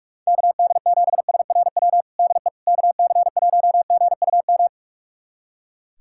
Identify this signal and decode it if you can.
Morse code, heard as a keyed tone, about 42 wpm, 700 Hz. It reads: KD7SRW DE KC1GUM
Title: USA Callsign DE Callsign - 6 Characters - 45wpm